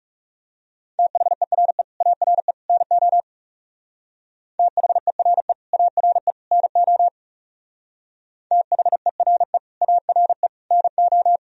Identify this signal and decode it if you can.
Morse code, heard as a keyed tone, about 40 wpm, 700 Hz. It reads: THERE ARE NO THERE ARE NO THERE ARE NO